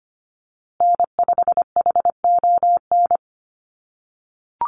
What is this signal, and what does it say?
Morse code, heard as a keyed tone, about 25 wpm, 700 Hz. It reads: N5HON
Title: USA Call Signs Repeated 2x 25wpm